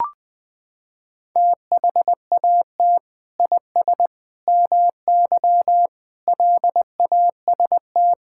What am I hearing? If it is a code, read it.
Morse code, heard as a keyed tone, about 20 wpm, 700 Hz. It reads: THAT IS MY LAST
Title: Sentences from Top 1600 Words 20wpm